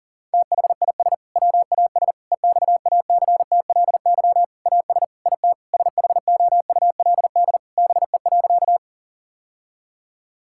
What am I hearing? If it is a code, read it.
Morse code, heard as a keyed tone, about 40 wpm, 700 Hz. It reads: THIS WAS EXACTLY AS IT SHOULD BE.
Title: Sentences from Top 1000 Words - Review Part 6 - 40wpm